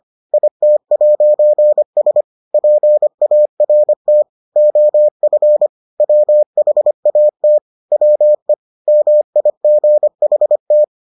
Morse code, heard as a keyed tone, about 25 wpm, 600 Hz. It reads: IT'S PART OF WHAT WE MIGHT